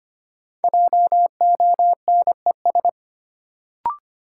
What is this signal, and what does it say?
Morse code, heard as a keyed tone, about 25 wpm, 700 Hz. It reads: JONES
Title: Sets of 5 Words - New Word Drill - 25wpm